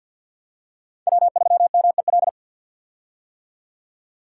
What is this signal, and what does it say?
Morse code, heard as a keyed tone, about 47 wpm, 690 Hz. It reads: W3MEL
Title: K1USN SST QSO Practice - 50wpm